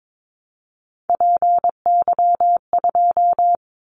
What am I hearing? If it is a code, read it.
Morse code, heard as a keyed tone, about 22 wpm, 700 Hz. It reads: PY2
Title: Callsign Prefixes for Top 50 DXCC - 22wpm